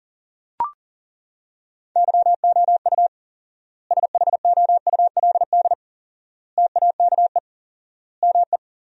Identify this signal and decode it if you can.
Morse code, heard as a keyed tone, about 40 wpm, 700 Hz. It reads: YOU SHOULD TAKE ME
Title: 4x Word-Spacing - Sentences from Top 300 Words - 40wpm